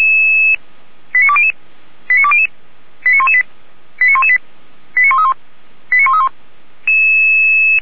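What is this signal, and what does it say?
Five-tone selective call (ZVEI), heard as quick runs of five tones, tones of about 70 ms. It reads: 89200 89200 89118 89118 89131 89131
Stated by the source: ZVEI five-tone selective call, received on SDR.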